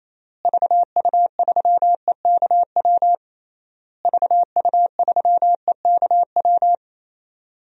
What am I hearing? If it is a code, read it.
Morse code, heard as a keyed tone, about 28 wpm, 700 Hz. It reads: VU3EKW VU3EKW
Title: DXCC-100 Call Signs - 6 Characters - Repeated 3x - 28wpm